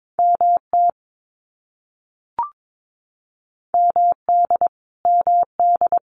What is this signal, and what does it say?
Morse code, heard as a keyed tone, about 22 wpm, 700 Hz. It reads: MT MD MD
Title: US States Repeated 3x 22wpm